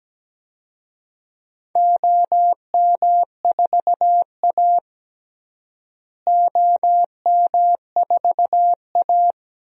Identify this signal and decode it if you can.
Morse code, heard as a keyed tone, about 17 wpm, 700 Hz. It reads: OM4A OM4A
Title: DXCC-100 Call Signs - 4 Characters - Repeated 2x - 17wpm